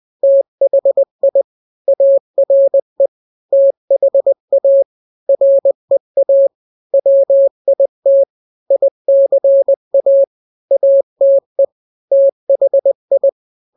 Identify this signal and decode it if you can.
Morse code, heard as a keyed tone, about 20 wpm, 550 Hz. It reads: THI ARE THA REA WIT ICA ATE THI
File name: n-grams / 3-grams 20 18 WPM